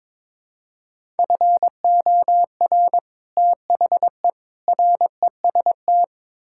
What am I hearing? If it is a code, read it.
Morse code, heard as a keyed tone, about 22 wpm, 700 Hz. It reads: FOR THE REST
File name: practice / 22wpm / Build A Phrase - 5-Word Phrases - 22wpm